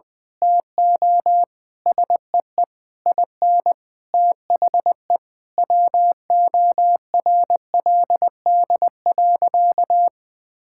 Morse code, heard as a keyed tone, about 20 wpm, 700 Hz. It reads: TO SEE IN THE WORLD.
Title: Sentences with 9-Word Phrases - Spoken Beginning - 20wpm